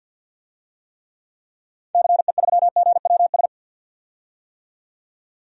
Morse code, heard as a keyed tone, about 52 wpm, 700 Hz. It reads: CE3GWS
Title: DXCC-100 Call Signs - 6 Characters - 50wpm